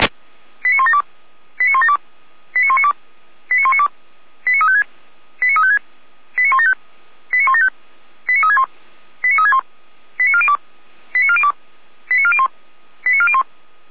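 Five-tone selective call (ZVEI), heard as quick runs of five tones, tones of about 70 ms. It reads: 89172 89172 89192 89192 89356 89356 89175 89175 89361 89361 89402 89402 89401 89401
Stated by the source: ZVEI five-tone selective call 89108, received by SDR, on the Feuerwehr Orlamünde audio server.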